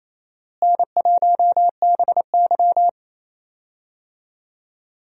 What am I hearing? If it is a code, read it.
Morse code, heard as a keyed tone, about 28 wpm, 700 Hz. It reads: N1BY
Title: USA Call Signs - 4 Characters - Repeated 2x - 28wpm